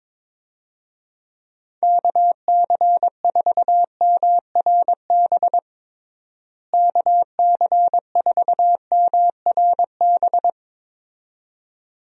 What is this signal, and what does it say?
Morse code, heard as a keyed tone, about 22 wpm, 700 Hz. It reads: KC4MRB KC4MRB
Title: USA Call Signs - 6 Characters - Repeated 2x - 22wpm